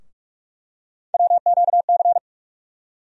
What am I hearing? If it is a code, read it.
Morse code, heard as a keyed tone, about 47 wpm, 700 Hz. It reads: WQC
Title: USA Call Sign Postfixes - 3 Letters 45wpm